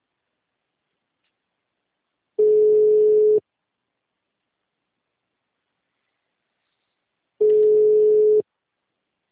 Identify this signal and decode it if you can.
Ringback tone (Europe, 425 Hz).
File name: out- -2004- - - .42636